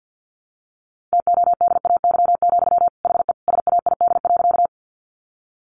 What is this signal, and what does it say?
Morse code, heard as a keyed tone, about 50 wpm, 700 Hz. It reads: TODAY, HE SAID.
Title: Sentences from Top 600 Words 50wpm